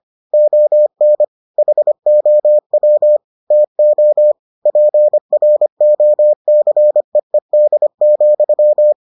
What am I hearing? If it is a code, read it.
Morse code, heard as a keyed tone, about 25 wpm, 600 Hz. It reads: ON HOW TO PROCEED,